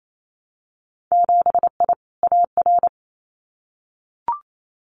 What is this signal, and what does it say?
Morse code, heard as a keyed tone, about 28 wpm, 700 Hz. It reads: 7I AR
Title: Winter Field Day Exchanges Encore 28wpm